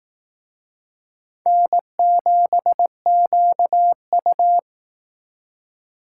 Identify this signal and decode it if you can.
Morse code, heard as a keyed tone, about 19 wpm, 700 Hz. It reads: N7QU